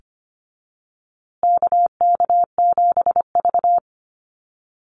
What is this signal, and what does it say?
Morse code, heard as a keyed tone, about 25 wpm, 700 Hz. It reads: KK7V
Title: USA Call Signs Repeated 2x 25wpm